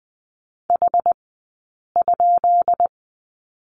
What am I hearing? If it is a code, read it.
Morse code, heard as a keyed tone, about 20 wpm, 700 Hz. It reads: H ?
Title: 165 Lesson 27d - Number 7 - Two Characters 20wpm